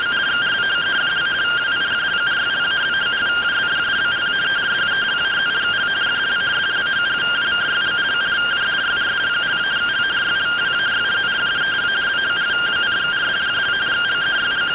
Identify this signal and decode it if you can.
Two-tone teleprinter signal, about 170 Hz shift.